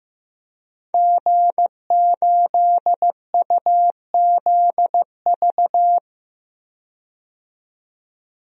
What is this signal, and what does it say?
Morse code, heard as a keyed tone, about 15 wpm, 700 Hz. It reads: G8UZV
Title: DXCC-100 Call Signs - 5 Characters - 15wpm